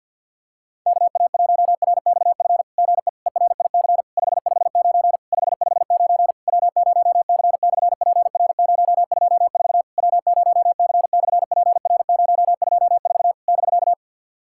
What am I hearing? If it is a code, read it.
Morse code, heard as a keyed tone, about 50 wpm, 700 Hz. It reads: KA1RXF GE ERIC 559 559 W0C/PR024 W0C/PR024 <BK>
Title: SOTA QSO Practice 50wpm